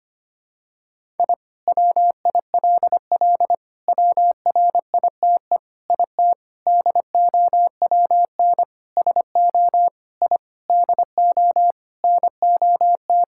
Morse code, heard as a keyed tone, about 25 wpm, 700 Hz. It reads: I WILL WRITE IT DOWN SO I DO NOT
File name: practice / 25wpm / Sentences with 4-Word Phrases - 25wpm